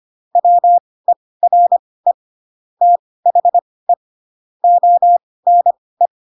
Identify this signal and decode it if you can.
Morse code, heard as a keyed tone, about 25 wpm, spaced as for 18 wpm, 700 Hz. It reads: WERE THE ONE
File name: Pinocchio / Pinocchio ch2 12 25 ik1hge